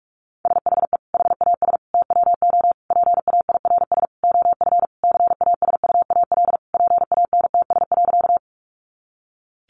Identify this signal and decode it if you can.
Morse code, heard as a keyed tone, about 44 wpm, 700 Hz. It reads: SHE HAS TWO PAIRS OF CASUAL PANTS.
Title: Additional Easy Phrases 45wpm